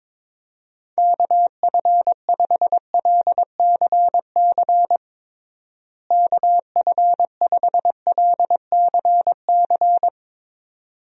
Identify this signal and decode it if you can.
Morse code, heard as a keyed tone, about 22 wpm, 700 Hz. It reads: KF5LCC KF5LCC
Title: USA Call Signs - 6 Characters - Repeated 2x - 22wpm